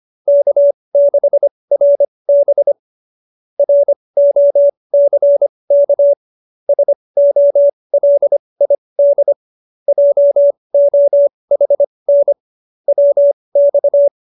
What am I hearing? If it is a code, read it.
Morse code, heard as a keyed tone, about 25 wpm, spaced as for 20 wpm, 570 Hz. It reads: K6RB ROCK SOLID JOHN WX
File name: QSO301 20